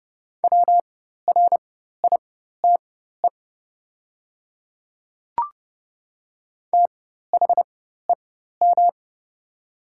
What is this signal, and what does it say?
Morse code, heard as a keyed tone, about 30 wpm, spaced as for 14 wpm, 700 Hz. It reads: WRITE THEM
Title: 4x Character Spacing - Top 100 Words - 30wpm